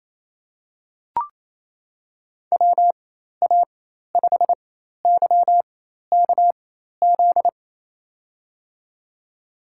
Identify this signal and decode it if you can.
Morse code, heard as a keyed tone, about 28 wpm, spaced as for 13 wpm, 700 Hz. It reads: WA5YKZ
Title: USA Call Signs - 4x Character Spacing - 28wpm